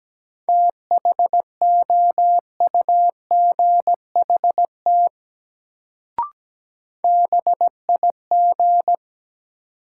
Morse code, heard as a keyed tone, about 17 wpm, 700 Hz. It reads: THOUGHT BIG
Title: Sets of 2 Words - New Word Drill - 17wpm